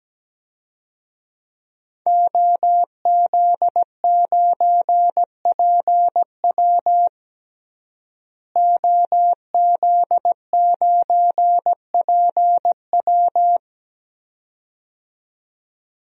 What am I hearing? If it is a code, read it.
Morse code, heard as a keyed tone, about 17 wpm, 700 Hz. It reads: OZ9PW OZ9PW
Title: DXCC-100 Call Signs - 5 Characters - Repeated 2x - 17wpm